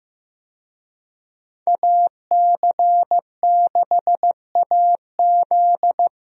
Morse code, heard as a keyed tone, about 15 wpm, 700 Hz. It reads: AC6AZ
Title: USA Call Signs - 5 Characters - Repeated 3x - 15wpm